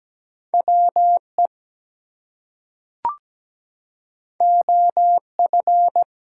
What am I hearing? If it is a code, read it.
Morse code, heard as a keyed tone, about 17 wpm, 700 Hz. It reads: WE OF
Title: Two Letter Words - 17wpm